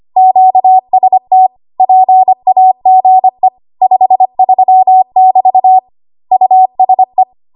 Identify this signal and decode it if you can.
Morse code, heard as a keyed tone, about 25 wpm, 750 Hz. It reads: QST PAGE 53 <BT> USE
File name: Archive / 25 WPM / 25WPM